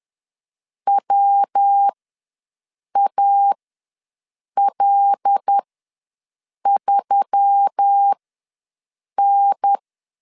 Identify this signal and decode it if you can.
Morse code, heard as a keyed tone, about 10 wpm, spaced as for 6 wpm, 800 Hz. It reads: WAL3N
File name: LEZIONI / 5 LBVC / 5 L ws 10wpm 3636